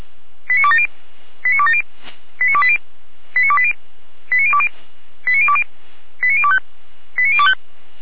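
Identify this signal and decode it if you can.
Five-tone selective call (ZVEI), heard as quick runs of five tones, tones of about 80 ms. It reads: 89270 89270 89280 89280 89020 89020 89025 89025
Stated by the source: ZVEI five-tone selective call 89208, received on an SDR.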